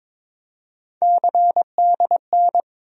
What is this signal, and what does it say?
Morse code, heard as a keyed tone, about 22 wpm, 700 Hz. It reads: CDN
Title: USA Call Sign Postfixes - 3 Letters 22wpm